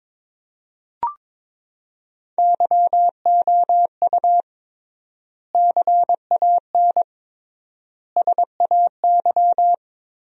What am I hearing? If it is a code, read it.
Morse code, heard as a keyed tone, about 22 wpm, 700 Hz. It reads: YOU CAN SAY
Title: 3x Word-Spacing - Sentences from Top 200 Words - 22wpm